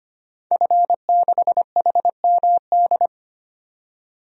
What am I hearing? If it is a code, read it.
Morse code, heard as a keyed tone, about 25 wpm, 700 Hz. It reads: F6HMD
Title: DXCC-100 Call Signs - 5 Characters - 25wpm